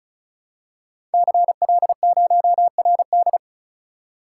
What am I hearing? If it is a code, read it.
Morse code, heard as a keyed tone, about 36 wpm, 700 Hz. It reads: CL0RD
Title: Code Groups - 5 Letter:Numbers - 35wpm